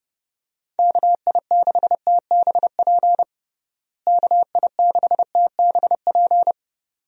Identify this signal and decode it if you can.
Morse code, heard as a keyed tone, about 32 wpm, 700 Hz. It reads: KI6TBP KI6TBP
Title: USA Call Signs - 6 Characters - Repeated 2x - 30wpm